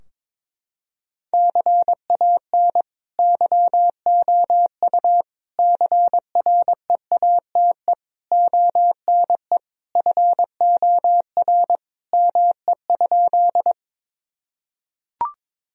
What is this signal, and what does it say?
Morse code, heard as a keyed tone, about 22 wpm, 700 Hz. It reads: CAN YOU CREATE ONE FOR ME?
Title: Sentences from Top 1000 Words - Review Part 3 - 22wpm